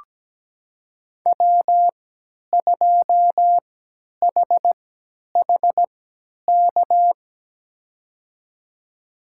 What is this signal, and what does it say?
Morse code, heard as a keyed tone, about 17 wpm, spaced as for 10 wpm, 700 Hz. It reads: W2HHK
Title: USA Call Signs - 5 Characters - 3x Character Spacing - 17wpm